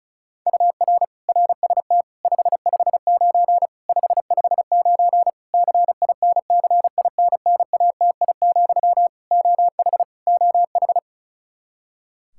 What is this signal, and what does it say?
Morse code, heard as a keyed tone, about 36 wpm, 700 Hz. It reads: UR RST 559 559 CINCINNATI, OH OH
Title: Practice QSO - Exchange 1 - Beginning - 35wpm